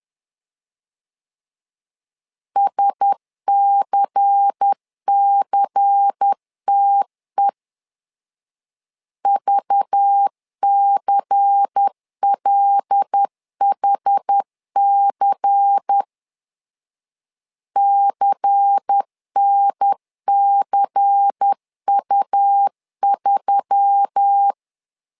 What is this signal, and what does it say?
Morse code, heard as a keyed tone, about 10 wpm, 800 Hz. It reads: SCCTE VCLHC CNCU3
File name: LEZIONI / 5 LBVC / 5 LBVC 10wpm 2349